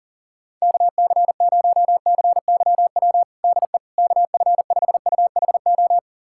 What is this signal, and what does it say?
Morse code, heard as a keyed tone, about 40 wpm, 700 Hz. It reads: KC0CYW DE KF5UHO
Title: USA Callsign DE Callsign - 6 Characters - 40wpm